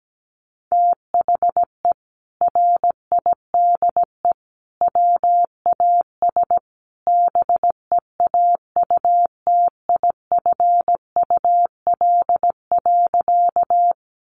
Morse code, heard as a keyed tone, about 17 wpm, 700 Hz. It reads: THE RIDE WAS BEAUTIFUL.
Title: Sentences from Top 1200 Words 17wpm